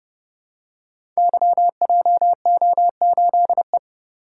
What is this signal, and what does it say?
Morse code, heard as a keyed tone, about 28 wpm, 700 Hz. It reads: YJO8E
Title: Code Groups - 5 Letter:Numbers - 30wpm